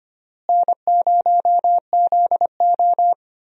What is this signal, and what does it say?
Morse code, heard as a keyed tone, about 25 wpm, 700 Hz. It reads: N0ZO
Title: USA Call Signs - 4 Characters - Repeated 3x - 25wpm